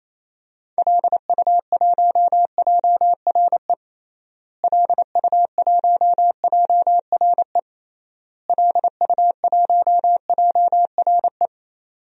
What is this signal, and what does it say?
Morse code, heard as a keyed tone, about 28 wpm, 700 Hz. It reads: LU1JRE LU1JRE LU1JRE
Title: DXCC-100 Call Signs - 6 Characters - Repeated 3x - 28wpm